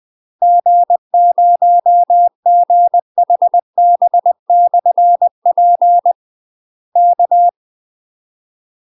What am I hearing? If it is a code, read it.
Morse code, heard as a keyed tone, about 20 wpm, 700 Hz. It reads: G0GHB/P K
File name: UK MASTER QSOs TRAINING / 20WPM / UK MASTER QSOs 0055